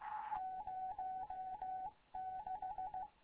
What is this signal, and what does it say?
Morse code, heard as a keyed tone, about 15 wpm, 750 Hz. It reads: 06
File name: Morze-HA2MN / hangp / 2 11 GYAKORLAT